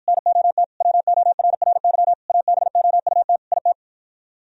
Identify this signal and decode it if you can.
Morse code, heard as a keyed tone, about 52 wpm, 700 Hz. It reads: NOT WORRY ABOUT IT